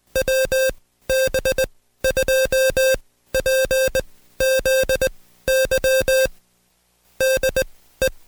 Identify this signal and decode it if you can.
Morse code, heard as a keyed tone, about 20 wpm, spaced as for 14 wpm, 530 Hz. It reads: WB2PZY DE
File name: Code Quick / AudioDownld2 / Seg133-Stepper6 13WPM